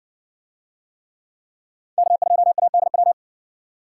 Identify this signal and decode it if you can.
Morse code, heard as a keyed tone, about 58 wpm, 700 Hz. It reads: K2ADW